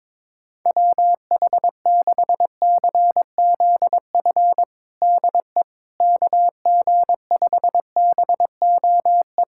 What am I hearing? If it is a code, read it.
Morse code, heard as a keyed tone, about 22 wpm, 700 Hz. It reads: WH6CZF DE KG5BOE
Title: USA Callsign DE Callsign - 6 Characters - 22wpm